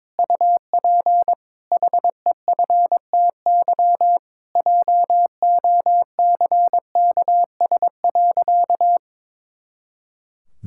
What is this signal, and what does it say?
Morse code, heard as a keyed tone, about 22 wpm, 700 Hz. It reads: UP HEFTY JOCKS.
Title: Pangram Challenge v2 - 22wpm